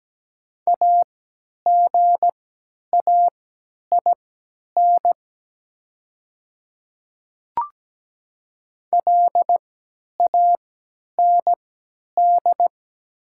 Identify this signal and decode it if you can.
Morse code, heard as a keyed tone, about 17 wpm, spaced as for 10 wpm, 700 Hz. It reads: AGAIN LAND
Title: 3x Character Spacing - Top 200 Words - 17wpm